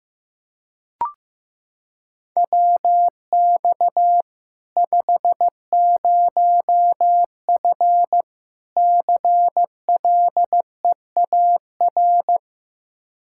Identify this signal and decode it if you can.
Morse code, heard as a keyed tone, about 15 wpm, 700 Hz. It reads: WX 50F CLEAR
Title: K1USN SST QSO Practice 15wpm